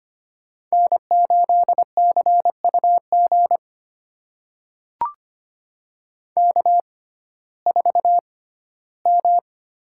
Morse code, heard as a keyed tone, about 25 wpm, spaced as for 15 wpm, 700 Hz. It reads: N8CUG K4M
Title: USA Call Signs - 5 Characters - 6x Character Spacing - 25wpm